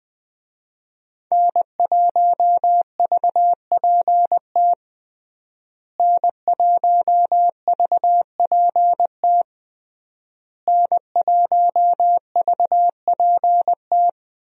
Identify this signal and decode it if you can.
Morse code, heard as a keyed tone, about 20 wpm, 700 Hz. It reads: N1VPT N1VPT N1VPT
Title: USA Call Signs - 5 Characters - Repeated 3x - 20wpm